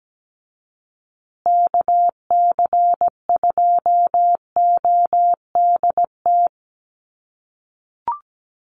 Morse code, heard as a keyed tone, about 17 wpm, 700 Hz. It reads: KC2ODT
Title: USA Call Signs Repeated 2x 17wpm